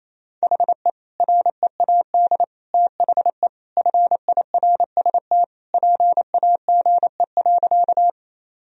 Morse code, heard as a keyed tone, about 28 wpm, 700 Hz. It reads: HE READ THE FIRST PAGE.
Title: Sentences from Top 1000 Words - Review Part 1 - 28wpm